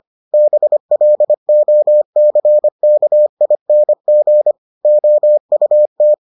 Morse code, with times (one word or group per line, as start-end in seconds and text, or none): 0.3-4.5 BLOCKING
4.8-6.1 OUT